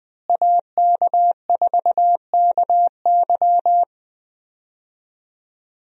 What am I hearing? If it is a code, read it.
Morse code, heard as a keyed tone, about 20 wpm, 700 Hz. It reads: AK4KY
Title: USA Call Signs - 5 Characters - Repeated 3x - 20wpm